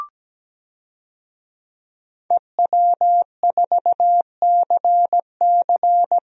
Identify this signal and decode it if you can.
Morse code, heard as a keyed tone, about 17 wpm, 700 Hz. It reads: EW4CC
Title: DXCC-100 Call Signs - 5 Characters - Repeated 3x - 17wpm